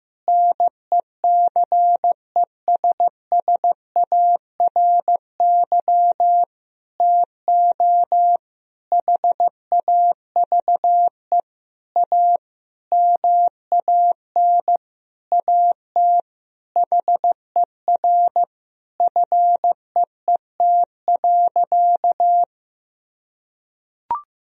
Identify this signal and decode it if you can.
Morse code, heard as a keyed tone, about 15 wpm, 700 Hz. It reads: NECESSARY TO HAVE A MAN AT HER FEET.
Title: Sentences from Top 1000 Words - Review Part 1 - 15wpm